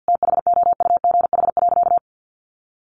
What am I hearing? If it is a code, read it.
Morse code, heard as a keyed tone, about 50 wpm, 700 Hz. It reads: THOUGH.
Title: Sentences from Top 1500 Words 50wpm